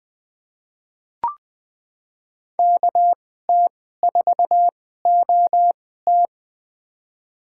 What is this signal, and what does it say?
Morse code, heard as a keyed tone, about 20 wpm, spaced as for 15 wpm, 700 Hz. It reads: KT4OT
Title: USA Call Signs - 5 Characters - 2x Character Spacing - 20wpm